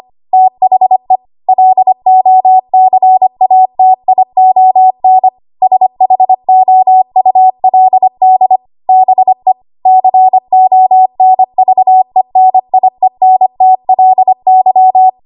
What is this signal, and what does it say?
Morse code, heard as a keyed tone, about 25 wpm, 750 Hz. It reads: THE LOCATION SHOULD BE CONVENIENTLY
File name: Morse / 25WPM